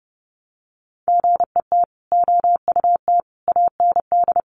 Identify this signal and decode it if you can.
Morse code, heard as a keyed tone, about 30 wpm, 700 Hz. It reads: GET OUT AND
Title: Sentences from Top 200 Words 30wpm